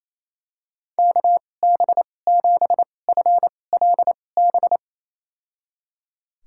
Morse code, heard as a keyed tone, about 28 wpm, spaced as for 20 wpm, 700 Hz. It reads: KB7FLB
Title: USA Call Signs - 2x Character Spacing - 28wpm